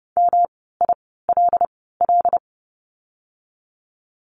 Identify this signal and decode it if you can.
Morse code, heard as a keyed tone, about 30 wpm, spaced as for 17 wpm, 700 Hz. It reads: MILL
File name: course / 30:10wpm / 108 Lesson 18f - Letter m - Words 30wpm